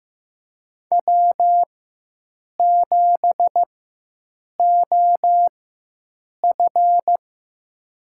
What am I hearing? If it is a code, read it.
Morse code, heard as a keyed tone, about 15 wpm, spaced as for 7 wpm, 700 Hz. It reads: W7OF
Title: USA Call Signs - 4 Characters - 4x Character Spacing - 15wpm